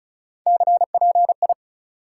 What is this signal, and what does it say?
Morse code, heard as a keyed tone, about 34 wpm, 700 Hz. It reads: CPI
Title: Top 180 QSO Elements 35wpm